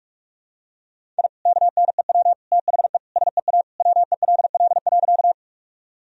Morse code, heard as a keyed tone, about 46 wpm, 700 Hz. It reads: I KNEW THE SEA WELL.
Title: Sentences from Top 1000 Words - Review Part 1 - 45wpm